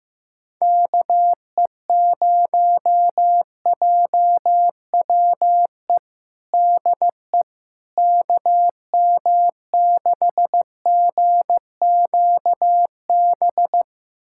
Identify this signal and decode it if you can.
Morse code, heard as a keyed tone, about 15 wpm, 700 Hz. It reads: KE0JWE DE KM6GQB
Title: USA Callsign DE Callsign 15wpm